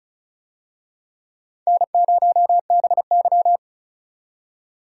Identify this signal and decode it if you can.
Morse code, heard as a keyed tone, about 36 wpm, 700 Hz. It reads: N0BY
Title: USA Call Signs - 4 Characters - 35wpm